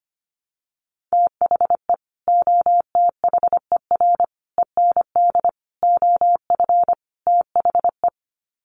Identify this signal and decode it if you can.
Morse code, heard as a keyed tone, about 25 wpm, 700 Hz. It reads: THE OTHER END OF THE
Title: Sets of 5 Words 25wpm